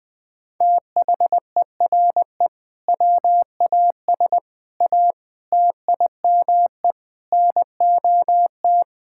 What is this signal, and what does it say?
Morse code, heard as a keyed tone, about 20 wpm, 700 Hz. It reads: THERE WAS A TIME NOT
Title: Sentences with 8-Word Phrases - 20wpm